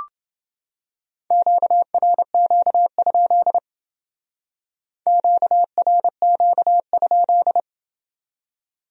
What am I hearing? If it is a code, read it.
Morse code, heard as a keyed tone, about 31 wpm, 700 Hz. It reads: QRQ? QRQ?